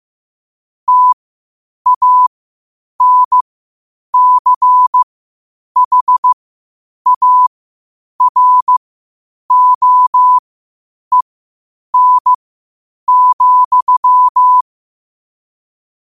Morse code, heard as a keyed tone, about 15 wpm, spaced as for 8 wpm, 1000 Hz. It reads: TANCHAROEN,
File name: qotdfiles / 8wpm